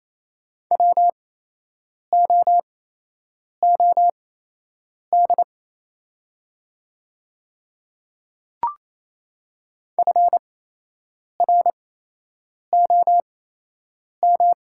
Morse code, heard as a keyed tone, about 28 wpm, spaced as for 8 wpm, 700 Hz. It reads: WOOD FROM
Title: 8x Character Spacing - Top 500 Words - 28wpm